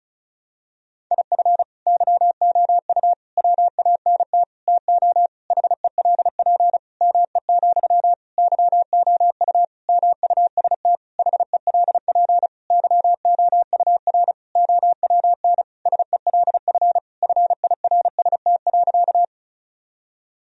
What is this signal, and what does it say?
Morse code, heard as a keyed tone, about 36 wpm, 700 Hz. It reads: IF YOU WANT TO HELP ME, YOU MUST HELP YOUR OWN SELF FIRST.
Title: Sentences from Top 1000 Words - Review Part 1 - 35wpm